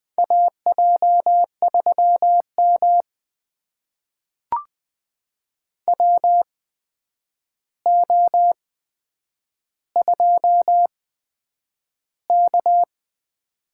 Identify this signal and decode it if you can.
Morse code, heard as a keyed tone, about 20 wpm, spaced as for 9 wpm, 700 Hz. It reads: AJ3M WO2K
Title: USA Call Signs - 4 Characters - 8x Character Spacing - 20wpm